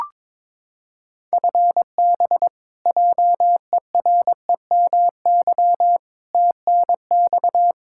Morse code, heard as a keyed tone, about 21 wpm, 700 Hz. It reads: FB JEREMY TNX
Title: Practice QSO - Exchange 3 - More Info - 22wpm